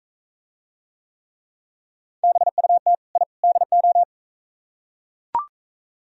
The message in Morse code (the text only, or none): BUT I DO